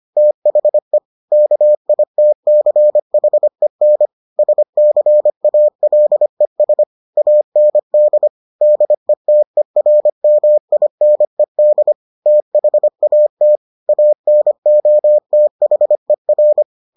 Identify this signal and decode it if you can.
Morse code, heard as a keyed tone, about 25 wpm, 600 Hz. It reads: THE KITCHEN SCALES AND DETERMINED THAT ANOTHER